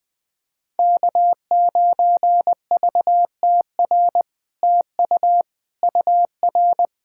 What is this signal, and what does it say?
Morse code, heard as a keyed tone, about 20 wpm, 700 Hz. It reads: K9VTR TU UR
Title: SOTA QSO Practice 20wpm